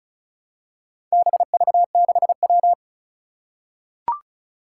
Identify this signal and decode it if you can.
Morse code, heard as a keyed tone, about 36 wpm, 700 Hz. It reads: BV6W
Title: Four Challenging Letter-Number Combinations 35wpm